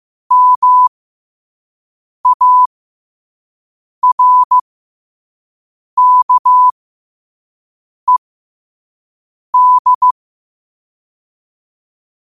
Morse code, heard as a keyed tone, about 15 wpm, spaced as for 5 wpm, 1000 Hz. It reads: MARKED
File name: qotdfiles / 5wpm